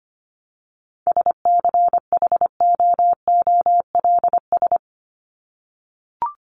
Morse code, heard as a keyed tone, about 25 wpm, 700 Hz. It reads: SCHOOLS
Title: ES:S Suffix - Top 100 Words with Base Words 25wpm